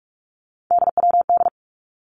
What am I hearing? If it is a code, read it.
Morse code, heard as a keyed tone, about 46 wpm, 700 Hz. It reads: DWD
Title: Three Challenging Letter-Number Combinations 45wpm